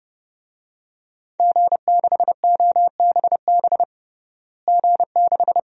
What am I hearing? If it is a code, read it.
Morse code, heard as a keyed tone, about 30 wpm, 700 Hz. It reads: G6OBB G6
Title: DXCC-100 Call Signs - 5 Characters - Repeated 2x - 30wpm